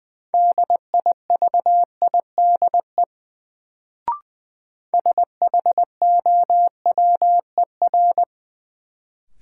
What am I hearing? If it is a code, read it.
Morse code, heard as a keyed tone, about 20 wpm, 700 Hz. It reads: DIVIDE SHOWER
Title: Sentences from Top 1600 Words - New Word Drill - 20wpm